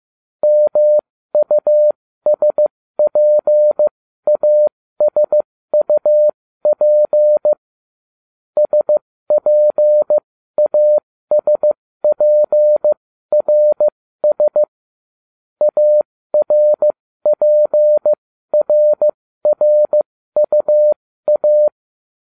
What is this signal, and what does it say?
Morse code, heard as a keyed tone, about 15 wpm, spaced as for 13 wpm, 600 Hz. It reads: MUSPASUP SPASPRS ARPRRUA